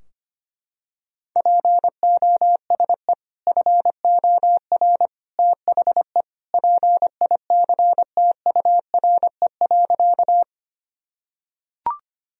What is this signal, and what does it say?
Morse code, heard as a keyed tone, about 25 wpm, 700 Hz. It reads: POSE FOR THE PICTURE.
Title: Sentences from Top 1000 Words - Review Part 2 - 25wpm